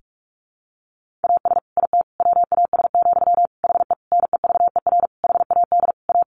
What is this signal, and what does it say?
Morse code, heard as a keyed tone, about 44 wpm, 700 Hz. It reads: AS IT WAS, HE NEVER HAD A
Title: Sentences from Top 1400 Words 45wpm